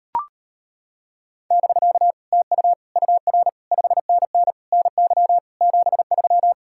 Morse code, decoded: <BK> TU UR 5NN NY 73

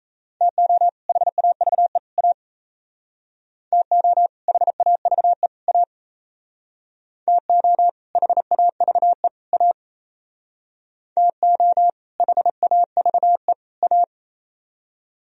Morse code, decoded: TO HAVE A TO HAVE A TO HAVE A TO HAVE A

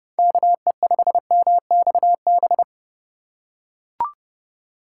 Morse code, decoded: KE5MXB

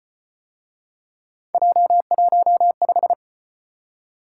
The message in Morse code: J15

